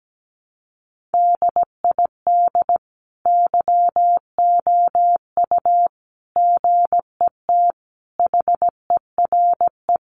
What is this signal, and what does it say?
Morse code, heard as a keyed tone, about 17 wpm, 700 Hz. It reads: DID YOU GET HERE
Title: Sentences from Top 700 Words 17wpm